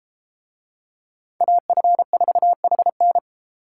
Morse code, heard as a keyed tone, about 32 wpm, 700 Hz. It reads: AF4HN